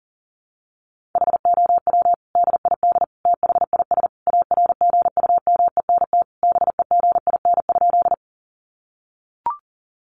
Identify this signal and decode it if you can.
Morse code, heard as a keyed tone, about 40 wpm, 700 Hz. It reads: HOW DID THIS ARGUMENT BEGIN?
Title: Sentences from Top 1400 Words 40wpm